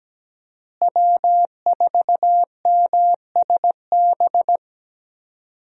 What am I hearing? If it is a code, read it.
Morse code, heard as a keyed tone, about 17 wpm, 700 Hz. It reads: W4MSB
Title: USA Call Signs - 5 Characters - Repeated 2x - 17wpm